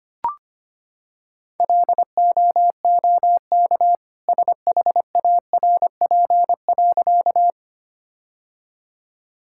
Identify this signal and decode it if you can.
Morse code, heard as a keyed tone, about 25 wpm, 700 Hz. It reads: LOOK SHARP.
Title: Sentences from Top 1600 Words 25wpm